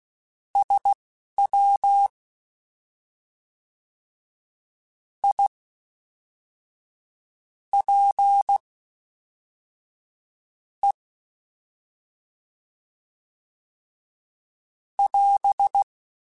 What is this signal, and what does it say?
Morse code, heard as a keyed tone, about 16 wpm, spaced as for 4 wpm, 800 Hz. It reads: SWIPE <AS>